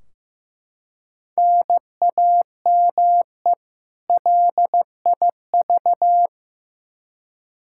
Morse code, heard as a keyed tone, about 15 wpm, 700 Hz. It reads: NAME LIV
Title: Practice QSO - Exchange 1 - Beginning - 15wpm